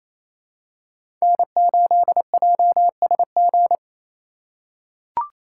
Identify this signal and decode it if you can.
Morse code, heard as a keyed tone, about 27 wpm, 700 Hz. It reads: N8JSG
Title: USA Call Signs - 5 Characters - 4x Character Spacing - 28wpm